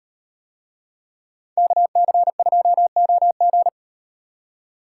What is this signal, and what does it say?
Morse code, heard as a keyed tone, about 36 wpm, 700 Hz. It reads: KC2OG